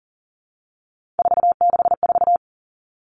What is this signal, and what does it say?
Morse code, heard as a keyed tone, about 37 wpm, 700 Hz. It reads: 464
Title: Three Challenging Letter-Number Combinations 40wpm